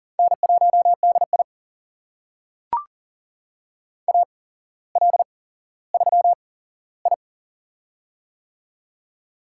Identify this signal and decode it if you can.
Morse code, heard as a keyed tone, about 40 wpm, spaced as for 17 wpm, 700 Hz. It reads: N1DI AL3I
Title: USA Call Signs - 4 Characters - 8x Character Spacing - 40wpm